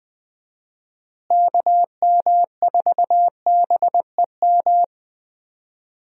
Morse code, heard as a keyed tone, about 20 wpm, 700 Hz. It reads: KM4BEM
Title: USA Call Signs - 6 Characters - Repeated 2x - 20wpm